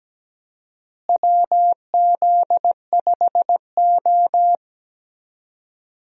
Morse code, heard as a keyed tone, about 17 wpm, 700 Hz. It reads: WZ5O
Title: USA Call Signs - 4 Characters - Repeated 3x - 17wpm